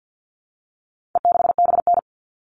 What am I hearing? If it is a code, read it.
Morse code, heard as a keyed tone, about 50 wpm, 700 Hz. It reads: E6BN
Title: All Cut Numbers - 4-Digit 50wpm